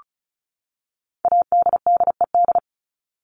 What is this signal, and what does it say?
Morse code, heard as a keyed tone, about 34 wpm, 700 Hz. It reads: ADDED
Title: ED Suffix - Top 100 Words with Base Words 35wpm